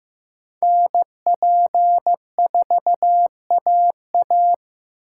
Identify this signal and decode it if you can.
Morse code, heard as a keyed tone, about 18 wpm, 700 Hz. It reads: NP4AA
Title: Speed Racing - USA 2x2 Call Signs 15wpm